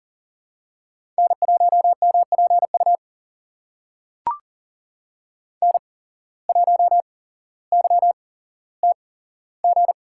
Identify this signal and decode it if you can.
Morse code, heard as a keyed tone, about 39 wpm, spaced as for 17 wpm, 700 Hz. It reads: N1MPU N1YTG
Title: USA Call Signs - 5 Characters - 8x Character Spacing - 40wpm